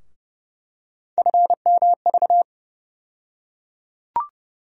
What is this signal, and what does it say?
Morse code, heard as a keyed tone, about 30 wpm, 700 Hz. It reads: FMV
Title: USA Call Sign Postfixes - 3 Letters 30wpm